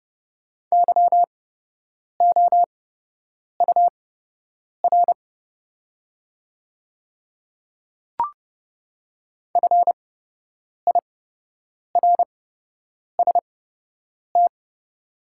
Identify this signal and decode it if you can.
Morse code, heard as a keyed tone, about 30 wpm, spaced as for 8 wpm, 700 Hz. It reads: YOUR FIRST
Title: 8x Character Spacing - Top 100 Words - 30wpm